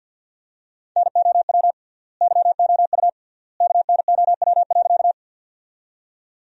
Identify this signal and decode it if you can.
Morse code, heard as a keyed tone, about 53 wpm, 700 Hz. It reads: NOW YOU KNOW.
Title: 3x Word-Spacing - Sentences from Top 100 Words - 50wpm